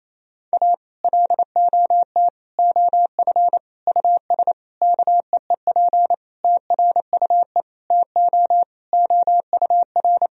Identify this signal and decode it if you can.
Morse code, heard as a keyed tone, about 28 wpm, 700 Hz. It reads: A LOT OF US KEEP TRUE TO OUR
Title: Sentences from Top 1000 Words - Review Part 3 - 28wpm